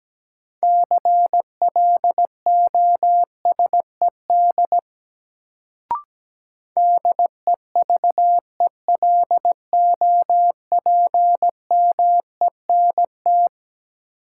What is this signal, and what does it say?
Morse code, heard as a keyed tone, about 17 wpm, 700 Hz. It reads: CLOSED DEVELOPMENT
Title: Sentences from Top 1400 Words - New Word Drill Alt - 17wpm